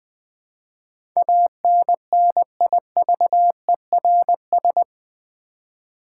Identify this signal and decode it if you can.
Morse code, heard as a keyed tone, about 20 wpm, 700 Hz. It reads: ANNIVERS
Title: Build A Word - 4 Letters At A Time - 11-Letter Words - 20wpm